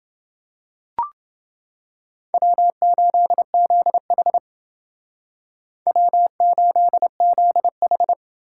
Morse code, heard as a keyed tone, about 28 wpm, 700 Hz. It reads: W8ZH W8ZH